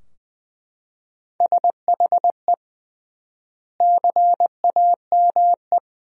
Morse code, heard as a keyed tone, about 20 wpm, 700 Hz. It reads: SHE CAME
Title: 3x Word-Spacing - Sentences from Top 300 Words - 20wpm